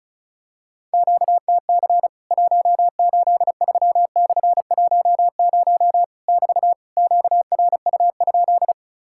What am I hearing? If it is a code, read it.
Morse code, heard as a keyed tone, about 34 wpm, 700 Hz. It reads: QTC 183/10 <BT> QRU?
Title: Worked All Europe DX Contest - QTC Exchange - 35wpm